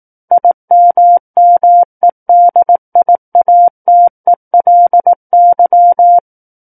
Morse code, heard as a keyed tone, about 18 wpm, 700 Hz. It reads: IMMEDIATELY
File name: 18 WPM